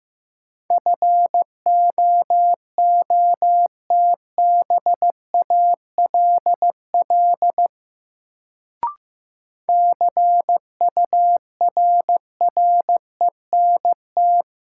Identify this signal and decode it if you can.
Morse code, heard as a keyed tone, about 15 wpm, 700 Hz. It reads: FOOTBALL CURRENT
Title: Sentences from Top 1600 Words - New Word Drill Alt - 15wpm